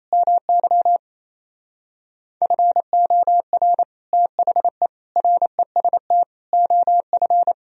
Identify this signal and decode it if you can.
Morse code, heard as a keyed tone, about 26 wpm, 700 Hz. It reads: MY FOR THE REST OF
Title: Speed-Racing - Sets of 5 Words 28wpm